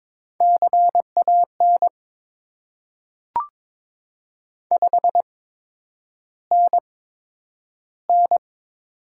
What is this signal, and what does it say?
Morse code, heard as a keyed tone, about 21 wpm, spaced as for 9 wpm, 700 Hz. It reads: CAN 5NN
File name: Top 90 QSO Elements - 8x Character Spacing - 22wpm